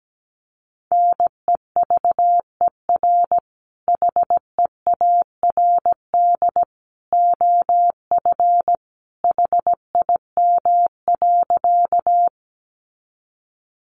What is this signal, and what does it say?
Morse code, heard as a keyed tone, about 17 wpm, 700 Hz. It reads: NEVER HEARD OF HIM.
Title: Sentences from Top 400 Words 17wpm